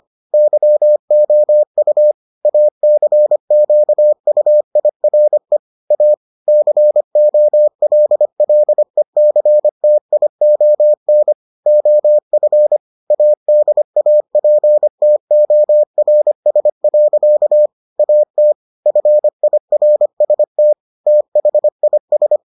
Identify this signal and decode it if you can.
Morse code, heard as a keyed tone, about 25 wpm, 600 Hz. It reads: YOU ACQUIRE A COLLECTION OF ADAPTORS. AT FIRST THIS